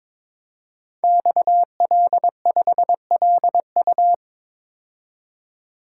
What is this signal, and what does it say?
Morse code, heard as a keyed tone, about 23 wpm, 700 Hz. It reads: XL5LU
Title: Code Groups - 5 Letter:Numbers - 22wpm